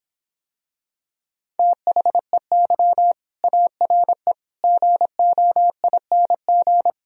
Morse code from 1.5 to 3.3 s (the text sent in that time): THEY